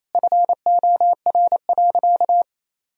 Morse code, heard as a keyed tone, about 28 wpm, 700 Hz. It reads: FOR.
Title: Sentences from Top 1000 Words - Review Part 4 - 28wpm